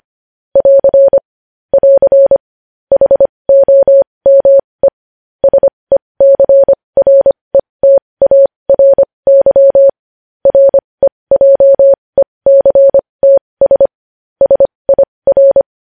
Morse code, heard as a keyed tone, about 25 wpm, spaced as for 20 wpm, 550 Hz. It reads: <AR> <AR> HOME SECRETARY REJECTS SIR
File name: podcast / News headlines-1x-25x20